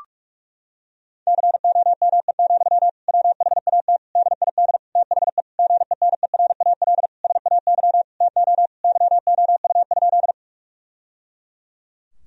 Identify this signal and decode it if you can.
Morse code, heard as a keyed tone, about 46 wpm, 700 Hz. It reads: COME, WHAT DID THE GENERAL SAY TO YOU?
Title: Sentences from Top 1000 Words - Review Part 3 - 45wpm